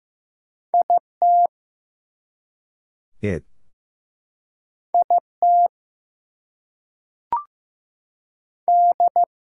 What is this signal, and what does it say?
Morse code, heard as a keyed tone, about 15 wpm, 700 Hz. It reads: IT IT D
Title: Two Letter Words - 15wpm